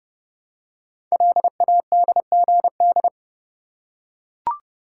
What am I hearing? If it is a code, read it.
Morse code, heard as a keyed tone, about 30 wpm, 700 Hz. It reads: LADGD
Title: Code Groups - 5 Letters - 30wpm